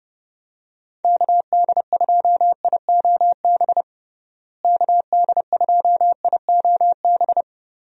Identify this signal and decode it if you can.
Morse code, heard as a keyed tone, about 32 wpm, 700 Hz. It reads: KD2IOB KD2IOB
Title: USA Call Signs - 6 Characters - Repeated 3x - 30wpm